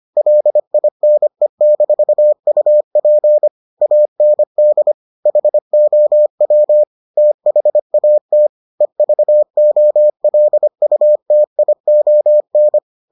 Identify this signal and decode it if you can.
Morse code, heard as a keyed tone, about 25 wpm, 600 Hz. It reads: LINE-UP AND HOW THAT EVOLUTION